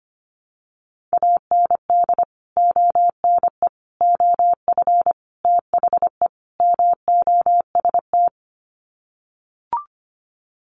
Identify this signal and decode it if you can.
Morse code, heard as a keyed tone, about 25 wpm, 700 Hz. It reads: AND ONE OF THE MOST
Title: Sets of 5 Words Encore 25wpm